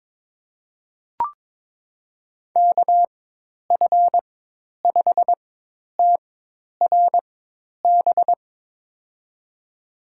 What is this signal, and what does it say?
Morse code, heard as a keyed tone, about 22 wpm, spaced as for 10 wpm, 700 Hz. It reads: KF5TRB
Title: USA Call Signs - 6 Characters - 4x Character Spacing - 22wpm